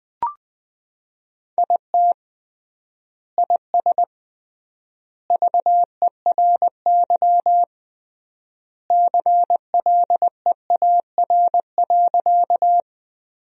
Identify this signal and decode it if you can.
Morse code, heard as a keyed tone, about 20 wpm, 700 Hz. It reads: IT IS VERY CLEAR.
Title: 3x Word-Spacing - Sentences from Top 500 Words - 20wpm